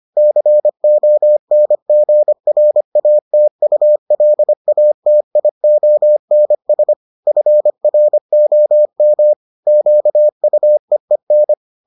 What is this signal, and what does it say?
Morse code, heard as a keyed tone, about 25 wpm, 600 Hz. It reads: CONGRATULATIONS FROM QUEEN